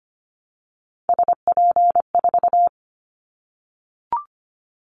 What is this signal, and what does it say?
Morse code, heard as a keyed tone, about 25 wpm, 700 Hz. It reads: SP4
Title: Callsign Prefixes for Top 25 DXCC - Repeated 3x - 25wpm